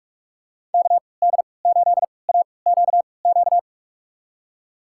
Morse code, heard as a keyed tone, about 46 wpm, spaced as for 26 wpm, 700 Hz. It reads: KD8AQQ
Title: USA Call Signs - 6 Characters - 3x Character Spacing - 45wpm